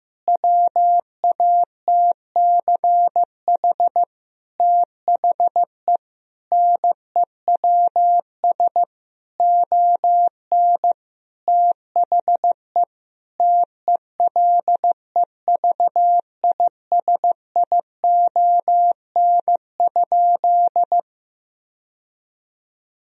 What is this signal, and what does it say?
Morse code, heard as a keyed tone, about 15 wpm, 700 Hz. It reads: WATCH THE NEWS ON THE TELEVISION?
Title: Sentences from Top 1000 Words - Review Part 4 - 15wpm